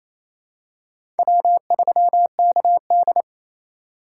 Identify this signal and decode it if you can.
Morse code, heard as a keyed tone, about 28 wpm, 700 Hz. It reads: W3KD
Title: USA Call Signs - 4 Characters - 3x Character Spacing - 28wpm